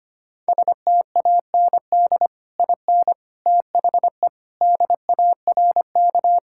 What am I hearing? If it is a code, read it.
Morse code, heard as a keyed tone, about 25 wpm, 700 Hz. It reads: STAND IN THE DARK
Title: Sentences from Top 1600 Words 25wpm